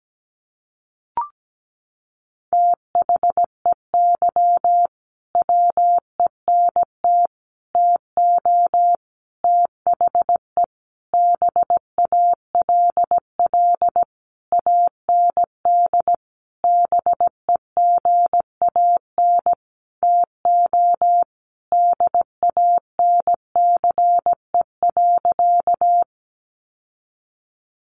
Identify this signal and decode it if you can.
Morse code, heard as a keyed tone, about 17 wpm, 700 Hz. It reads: THEY WENT TO THE BALL AND BEGAN TO DANCE.
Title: Sentences from Top 900 Words 17wpm